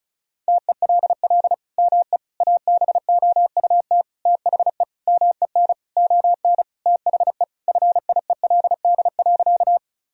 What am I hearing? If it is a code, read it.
Morse code, heard as a keyed tone, about 36 wpm, 700 Hz. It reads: TELL ME ABOUT THE MEN ON THE FIELD.
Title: Sentences from Top 1000 Words - Review Part 2 - 35wpm